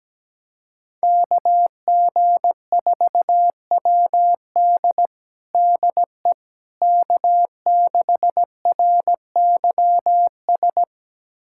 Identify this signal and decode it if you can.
Morse code, heard as a keyed tone, about 17 wpm, 700 Hz. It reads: KG4WD DE K6RYS
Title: USA Callsign DE Callsign - 5 Characters - 17wpm